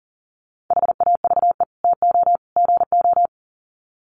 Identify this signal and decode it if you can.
Morse code, heard as a keyed tone, about 38 wpm, 700 Hz. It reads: HAVE TO GO
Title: Sets of 3 Words 40wpm